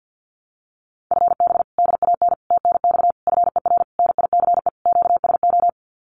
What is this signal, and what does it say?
Morse code, heard as a keyed tone, about 50 wpm, 700 Hz. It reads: FB DAN TNX FER NICE QSO
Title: Common QSO Phrases Encore 50wpm